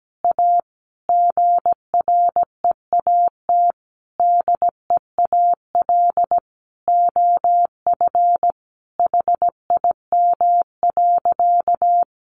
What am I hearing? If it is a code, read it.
Morse code, heard as a keyed tone, about 17 wpm, 700 Hz. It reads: A GREAT DEAL OF HIM.
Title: Sentences from Top 800 Words 17wpm